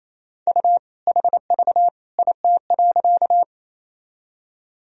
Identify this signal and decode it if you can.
Morse code, heard as a keyed tone, about 28 wpm, 700 Hz. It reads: U HV IT.
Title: Sentences from Top 100 Words and Abbreviations 28wpm